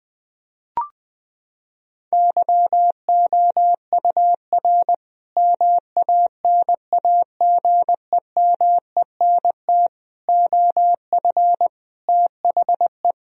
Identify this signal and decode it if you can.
Morse code, heard as a keyed tone, about 20 wpm, 700 Hz. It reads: YOUR MANAGEMENT OF THE
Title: Sentences from Top 1000 Words - Review Part 4 - 20wpm